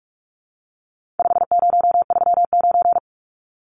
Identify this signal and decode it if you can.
Morse code, heard as a keyed tone, about 44 wpm, 700 Hz. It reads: 5039
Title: Numbers - 4-Digit 45wpm